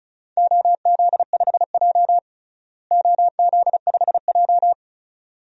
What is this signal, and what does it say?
Morse code, heard as a keyed tone, about 35 wpm, 700 Hz. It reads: OZ5J OZ5J
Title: DXCC-100 Call Signs - 4 Characters - Repeated 2x - 35wpm